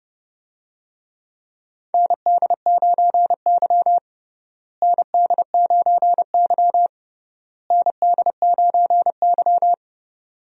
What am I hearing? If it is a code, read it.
Morse code, heard as a keyed tone, about 30 wpm, 700 Hz. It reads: ND9Y ND9Y ND9Y
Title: USA Call Signs - 4 Characters - Repeated 3x - 30wpm